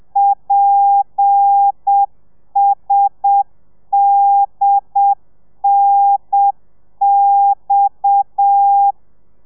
Morse code, heard as a keyed tone, about 7 wpm, 800 Hz. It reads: PSDNX
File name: telegrafia / i2tfj / Lez7 30